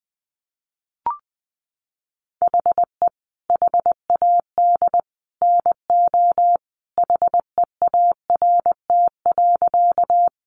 Morse code, heard as a keyed tone, about 20 wpm, 700 Hz. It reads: HE HAD NO HEART.
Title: Sentences from Top 1200 Words 20wpm